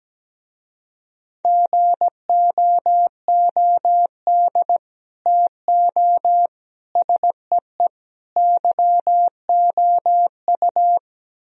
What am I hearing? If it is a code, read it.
Morse code, heard as a keyed tone, about 17 wpm, 700 Hz. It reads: GOOD TO SEE YOU
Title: Speed-Racing - Sets of 4 Words 17wpm